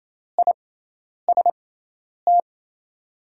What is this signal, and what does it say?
Morse code, heard as a keyed tone, about 28 wpm, spaced as for 10 wpm, 700 Hz. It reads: IST
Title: USA Call Signs - 5 Characters - 6x Character Spacing - 28wpm